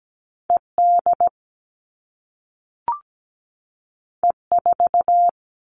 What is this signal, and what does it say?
Morse code, heard as a keyed tone, about 17 wpm, 700 Hz. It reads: ED E4
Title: All Cut Numbers - 2-Digit 17wpm